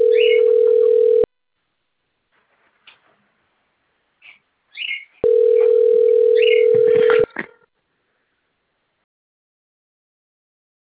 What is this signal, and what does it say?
Ringback tone (North America), 440 + 480 Hz.